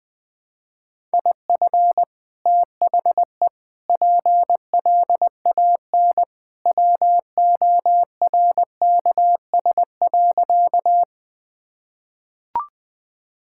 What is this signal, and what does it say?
Morse code, heard as a keyed tone, about 20 wpm, 700 Hz. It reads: IF THE PLAN WORKS.
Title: Sentences with 2-Word Phrases - Spoken Beginning - 20wpm